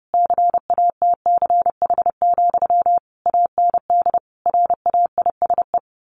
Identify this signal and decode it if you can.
Morse code, heard as a keyed tone, about 28 wpm, 700 Hz. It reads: CATCH, AND RAISE
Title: A Sound Of Thunder 30wpm